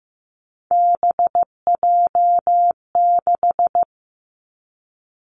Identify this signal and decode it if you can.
Morse code, heard as a keyed tone, about 15 wpm, 700 Hz. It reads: BJ6
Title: Three Challenging Letter-Number Combinations 15wpm